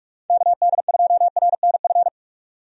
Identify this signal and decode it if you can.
Morse code, heard as a keyed tone, about 46 wpm, 700 Hz. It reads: KD2RNF
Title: USA Call Signs - 6 Characters - 45wpm